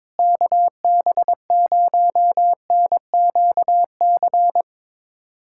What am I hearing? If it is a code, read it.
Morse code, heard as a keyed tone, about 22 wpm, 700 Hz. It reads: KB0NQC
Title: USA Callsign DE Callsign - 6 Characters - 22wpm